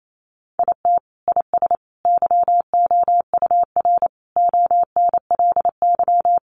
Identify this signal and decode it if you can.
Morse code, heard as a keyed tone, about 28 wpm, 700 Hz. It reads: IT IS YOUR ONLY
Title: Sentences from Top 1000 Words 28wpm